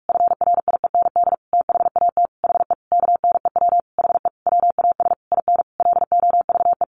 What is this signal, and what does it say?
Morse code, heard as a keyed tone, about 44 wpm, 700 Hz. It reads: FRIEND THAT HE KNEW HE WAS IN LOVE
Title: Sentences from Top 1300 Words 45wpm